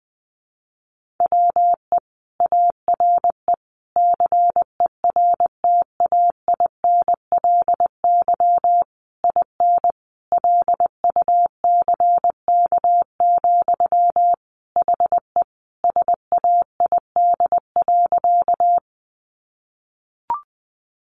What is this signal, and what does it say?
Morse code, heard as a keyed tone, about 20 wpm, 700 Hz. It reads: WE ARE CERTAINLY IN LUCK, HE SAID.
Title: Sentences from Top 1500 Words 20wpm